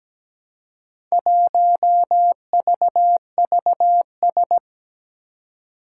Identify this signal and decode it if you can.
Morse code, heard as a keyed tone, about 17 wpm, 700 Hz. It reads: 1VVS
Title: Four Challenging Letter-Number Combinations 17wpm